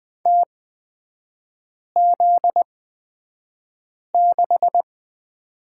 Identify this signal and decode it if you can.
Morse code, heard as a keyed tone, about 20 wpm, spaced as for 5 wpm, 700 Hz. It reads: TZ6
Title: 255 Lesson 41e - Period - Three Characters 20wpm